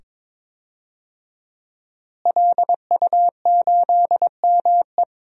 Morse code, heard as a keyed tone, about 22 wpm, 700 Hz. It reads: LU8ME
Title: DXCC-100 Call Signs - 5 Characters - Repeated 2x - 22wpm